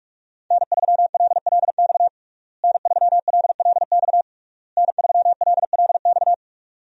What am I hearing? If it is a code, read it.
Morse code, heard as a keyed tone, about 46 wpm, 700 Hz. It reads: N3LLX N3LLX N3LLX
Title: USA Call Signs - 5 Characters - Repeated 3x - 45wpm